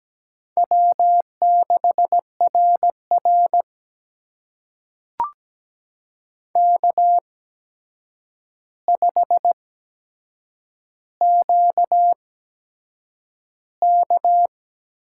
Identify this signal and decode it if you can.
Morse code, heard as a keyed tone, about 17 wpm, spaced as for 7 wpm, 700 Hz. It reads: W6RR K5QK
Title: USA Call Signs - 4 Characters - 8x Character Spacing - 17wpm